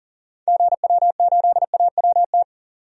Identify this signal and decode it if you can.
Morse code, heard as a keyed tone, about 40 wpm, 700 Hz. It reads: GW8AWT
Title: DXCC-100 Call Signs - 6 Characters - 40wpm